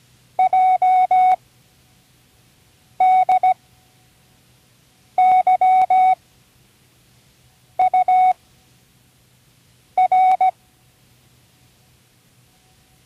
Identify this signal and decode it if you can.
Morse code, heard as a keyed tone, about 17 wpm, spaced as for 5 wpm, 740 Hz. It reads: JDYUR
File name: Learning Morse Code (from Tune in the World)